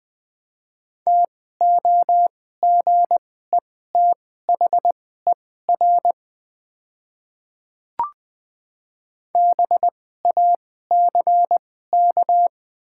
Morse code, heard as a keyed tone, about 20 wpm, spaced as for 14 wpm, 700 Hz. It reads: TOGETHER BACK
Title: 2x Character Spacing - Top 500 Words - 20wpm